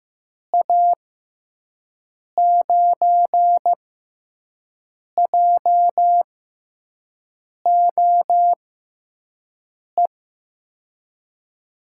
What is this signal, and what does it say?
Morse code, heard as a keyed tone, about 16 wpm, spaced as for 5 wpm, 700 Hz. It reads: A9JOE
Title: USA Call Signs - 6 Characters - 6x Character Spacing - 15wpm